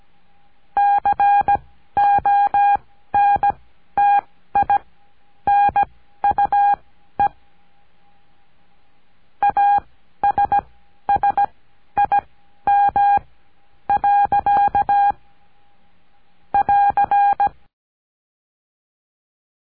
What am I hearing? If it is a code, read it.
Morse code, heard as a keyed tone, about 17 wpm, spaced as for 11 wpm, 810 Hz. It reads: CONTINUE ASSIM. <AR>
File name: cursos / CURSO DE TELEGRAFIA / Telegrafia - Trilha 17